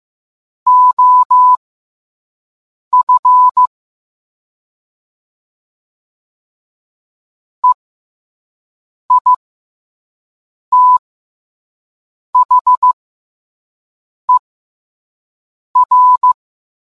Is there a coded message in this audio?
Yes: Morse code from 0.7 to 16.3 s, heard as a keyed tone, about 15 wpm, spaced as for 5 wpm, 1000 Hz. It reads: OF EITHER